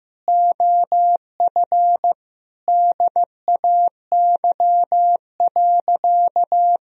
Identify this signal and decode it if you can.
Morse code, heard as a keyed tone, about 15 wpm, 700 Hz. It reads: OF DAY.
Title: Sentences from Top 1000 Words - Review Part 4 - 15wpm